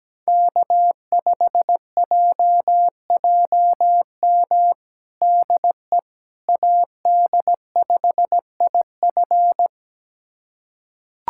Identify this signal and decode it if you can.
Morse code, heard as a keyed tone, about 17 wpm, 700 Hz. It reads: K5JJM DE AD5IF
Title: USA Callsign DE Callsign - 5 Characters - 17wpm